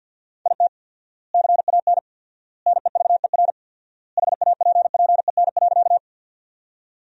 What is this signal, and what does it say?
Morse code, heard as a keyed tone, about 53 wpm, 700 Hz. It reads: IT CAN NEVER HAPPEN.
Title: 4x Word-Spacing - Sentences from Top 400 Words - 50wpm